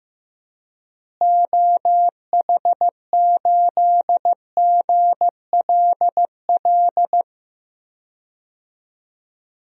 Morse code, heard as a keyed tone, about 15 wpm, 700 Hz. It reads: OH8GLL
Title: DXCC-100 Call Signs - 6 Characters - Repeated 3x - 15wpm